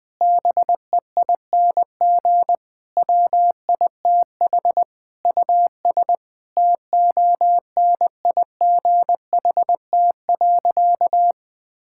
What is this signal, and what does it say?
Morse code, heard as a keyed tone, about 20 wpm, 700 Hz. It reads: BEING WITH US TONIGHT.
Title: Sentences with 5-Word Phrases - 20wpm